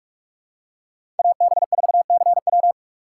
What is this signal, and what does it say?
Morse code, heard as a keyed tone, about 46 wpm, 700 Hz. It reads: AB4CW
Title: USA Call Signs - 5 Characters - 6x Character Spacing - 45wpm